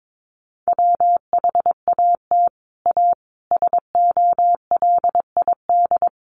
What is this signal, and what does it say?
Morse code, heard as a keyed tone, about 22 wpm, 700 Hz. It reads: WHAT A SOLID
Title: Sentences from Top 1500 Words 22wpm